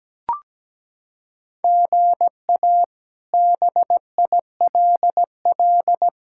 Morse code, heard as a keyed tone, about 17 wpm, 700 Hz. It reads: GA BILL
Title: K1USN SST QSO Practice - 17wpm